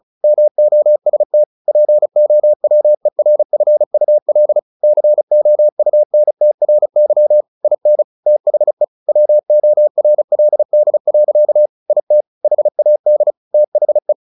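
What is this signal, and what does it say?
Morse code, heard as a keyed tone, about 34 wpm, 600 Hz. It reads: MOST POWERFUL COUNTRY IN THE WORLD. IT HAD THE
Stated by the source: Morse practice at 35 wpm from CW med Gnister, sending Ken Follett's novel Never in English.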